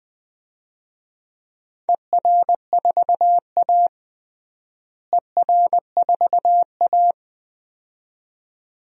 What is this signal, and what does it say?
Morse code, heard as a keyed tone, about 20 wpm, 700 Hz. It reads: ER4A ER4A
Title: DXCC-100 Call Signs - 4 Characters - Repeated 2x - 20wpm